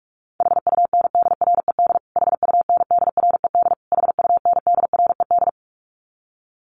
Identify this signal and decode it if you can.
Morse code, heard as a keyed tone, about 44 wpm, 700 Hz. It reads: HUNDRED HUNDRED HUNDRED
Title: Top 500 Words Repeated 45wpm